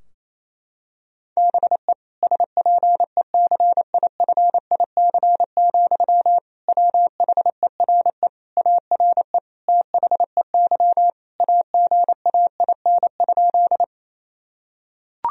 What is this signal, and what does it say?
Morse code, heard as a keyed tone, about 28 wpm, 700 Hz. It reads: BE SPECIFIC, WHERE ARE THEY AGAIN?
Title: Sentences from Top 1000 Words - Review Part 3 - 28wpm